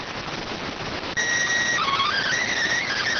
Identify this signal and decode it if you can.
SSTV (slow-scan television), PD 290.